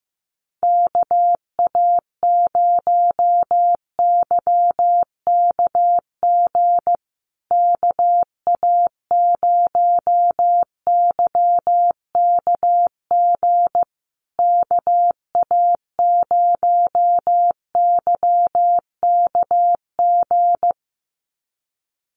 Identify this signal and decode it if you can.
Morse code, heard as a keyed tone, about 14 wpm, 700 Hz. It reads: KA0YKG KA0YKG KA0YKG
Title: USA Call Signs Repeated 3x 15wpm